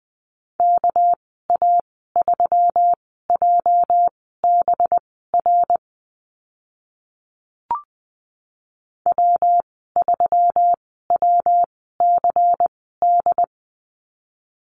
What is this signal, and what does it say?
Morse code, heard as a keyed tone, about 20 wpm, spaced as for 14 wpm, 700 Hz. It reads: KA3JBR W3WCD
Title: 219 Lesson 35g - Number 0 - Call Signs 20wpm